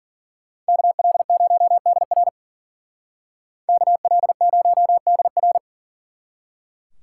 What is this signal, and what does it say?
Morse code, heard as a keyed tone, about 48 wpm, 700 Hz. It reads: KL0DR KL0DR
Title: Speed Racing - USA 2x2 Call Signs 40wpm